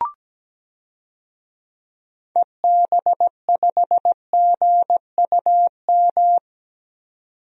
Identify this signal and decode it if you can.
Morse code, heard as a keyed tone, about 17 wpm, 700 Hz. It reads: EB5GUM
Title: DXCC-100 Call Signs - 6 Characters - 17wpm